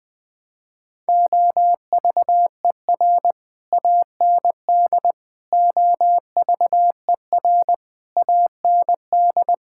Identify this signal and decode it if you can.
Morse code, heard as a keyed tone, about 20 wpm, 700 Hz. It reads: OVER AND OVER AND
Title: Sets of 6 Words - 20wpm